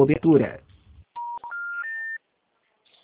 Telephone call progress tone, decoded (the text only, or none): special information tone